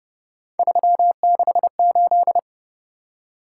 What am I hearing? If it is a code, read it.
Morse code, heard as a keyed tone, about 29 wpm, 700 Hz. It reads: 368
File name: Speed-Racing - Three Letter-Number Combinations - 25wpm